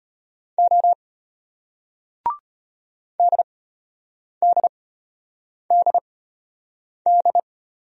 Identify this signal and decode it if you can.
Morse code, heard as a keyed tone, about 34 wpm, spaced as for 8 wpm, 700 Hz. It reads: O DDDD